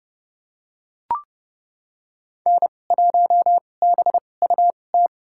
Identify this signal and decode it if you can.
Morse code, heard as a keyed tone, about 32 wpm, spaced as for 22 wpm, 700 Hz. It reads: N1BUT
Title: USA Call Signs - 5 Characters - 2x Character Spacing - 30wpm